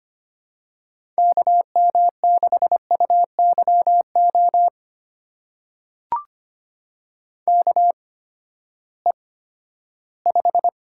Morse code, heard as a keyed tone, about 25 wpm, spaced as for 14 wpm, 700 Hz. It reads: KM6UYO KE5
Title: USA Call Signs - 6 Characters - 8x Character Spacing - 25wpm